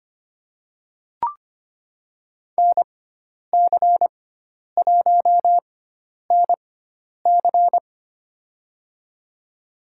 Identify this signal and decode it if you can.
Morse code, heard as a keyed tone, about 25 wpm, spaced as for 10 wpm, 700 Hz. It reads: NC1NC
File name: Lesson 16g - Letter c - Call Signs 25wpm